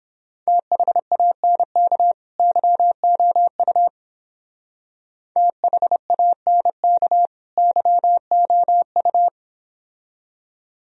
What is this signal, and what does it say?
Morse code, heard as a keyed tone, about 29 wpm, 700 Hz. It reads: THANK YOU THANK YOU